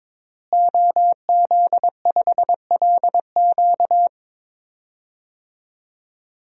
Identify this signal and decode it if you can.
Morse code, heard as a keyed tone, about 22 wpm, 700 Hz. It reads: OZ5LQ
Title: DXCC-100 Call Signs - 5 Characters - Repeated 3x - 22wpm